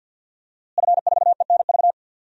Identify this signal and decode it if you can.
Morse code, heard as a keyed tone, about 52 wpm, 700 Hz. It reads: F4ENV
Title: DXCC-100 Call Signs - 5 Characters - Repeated 2x - 50wpm